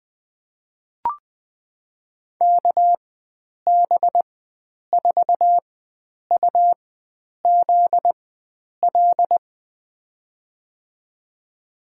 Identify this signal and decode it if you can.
Morse code, heard as a keyed tone, about 21 wpm, spaced as for 9 wpm, 700 Hz. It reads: KB4UZL
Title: USA Call Signs - 6 Characters - 4x Character Spacing - 20wpm